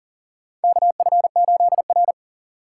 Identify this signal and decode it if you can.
Morse code, heard as a keyed tone, about 38 wpm, 700 Hz. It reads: KF8R
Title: USA Call Signs - 4 Characters - 4x Character Spacing - 40wpm